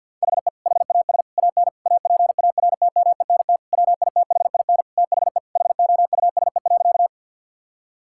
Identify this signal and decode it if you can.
Morse code, heard as a keyed tone, about 50 wpm, 700 Hz. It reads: HE HAS AN APARTMENT WITHIN THE HOUSE.
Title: Sentences from Top 1000 Words - Review Part 6 - 50wpm